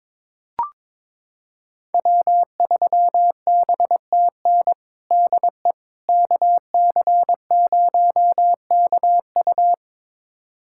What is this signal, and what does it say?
Morse code, heard as a keyed tone, about 22 wpm, 700 Hz. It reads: W3BTN DE KC0KU
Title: USA Callsign DE Callsign - 5 Characters - 22wpm